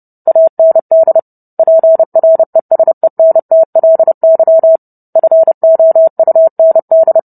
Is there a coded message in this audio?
Yes: Morse code from 0.3 to 7.2 s, heard as a keyed tone, about 30 wpm, 650 Hz. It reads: AND PRESENTLY FOUND